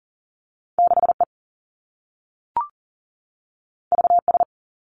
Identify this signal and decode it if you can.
Morse code, heard as a keyed tone, about 42 wpm, 700 Hz. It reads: 6E VS
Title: Two Challenging Letter-Number Combinations 40wpm